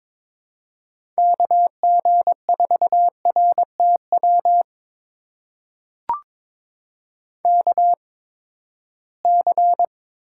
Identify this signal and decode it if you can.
Morse code, heard as a keyed tone, about 21 wpm, spaced as for 15 wpm, 700 Hz. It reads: KG4RTW KC
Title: USA Call Signs - 6 Characters - 8x Character Spacing - 22wpm